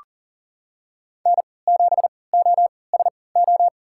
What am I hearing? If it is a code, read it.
Morse code, heard as a keyed tone, about 40 wpm, spaced as for 23 wpm, 700 Hz. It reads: N7OSO
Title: USA Call Signs - 5 Characters - 3x Character Spacing - 40wpm